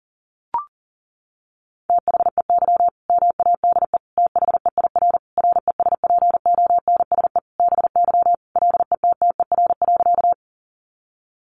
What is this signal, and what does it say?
Morse code, heard as a keyed tone, about 40 wpm, 700 Hz. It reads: THEY MADE THEIR RESPONSE BY LETTER.
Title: Sentences from Top 1400 Words 40wpm